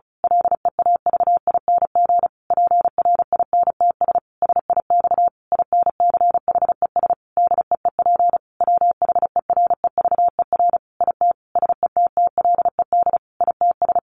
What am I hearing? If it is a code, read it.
Morse code, heard as a keyed tone, about 34 wpm, 700 Hz. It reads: LEAVING PRINTS SIX INCHES DEEP WHEREVER IT SETTLED ITS
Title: A Sound Of Thunder 35wpm